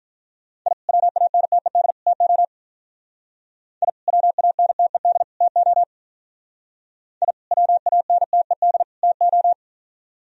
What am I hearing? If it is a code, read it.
Morse code, heard as a keyed tone, about 43 wpm, 700 Hz. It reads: I WANTED TO I WANTED TO I WANTED TO
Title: Speed-Racing - Sets of 3 Words 35wpm